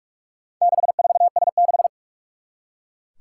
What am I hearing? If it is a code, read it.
Morse code, heard as a keyed tone, about 44 wpm, 700 Hz. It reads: 64S6
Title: Four Challenging Letter-Number Combinations 45wpm